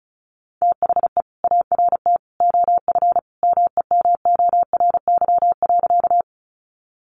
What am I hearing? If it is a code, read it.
Morse code, heard as a keyed tone, about 34 wpm, 700 Hz. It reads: THE ART OF MEMORY.
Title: Sentences from Top 1300 Words 35wpm